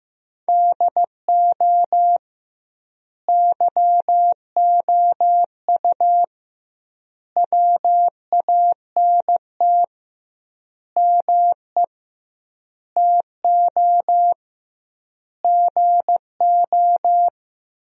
Morse code, heard as a keyed tone, about 15 wpm, 700 Hz. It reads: DO YOU WANT ME TO GO
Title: 2x Word-Spacing - Sentences from Top 200 Words - 15wpm